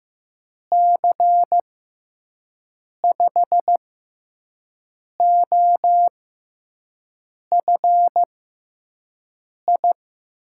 Morse code, heard as a keyed tone, about 15 wpm, spaced as for 5 wpm, 700 Hz. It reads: C5OFI
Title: USA Call Signs - 6 Characters - 6x Character Spacing - 15wpm